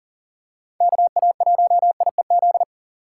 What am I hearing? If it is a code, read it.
Morse code, heard as a keyed tone, about 40 wpm, 700 Hz. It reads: KA1IEZ
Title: USA Call Signs - 6 Characters - 6x Character Spacing - 40wpm